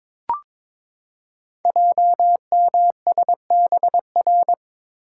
Morse code, heard as a keyed tone, about 22 wpm, 700 Hz. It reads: JMSBR
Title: Code Groups - 5 Letters - 22wpm